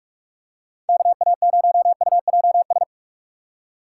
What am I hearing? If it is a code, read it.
Morse code, heard as a keyed tone, about 46 wpm, 700 Hz. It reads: KA0UJS
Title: USA Call Signs - 6 Characters - 6x Character Spacing - 45wpm